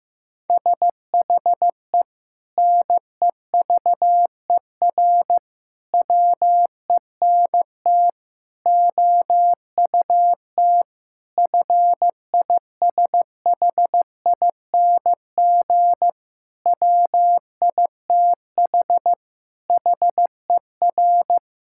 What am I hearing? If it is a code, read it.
Morse code, heard as a keyed tone, about 15 wpm, 700 Hz. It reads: SHE NEVER WENT OUT FISHING WITH HER
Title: Sentences from Top 1000 Words - Review Part 5 - 15wpm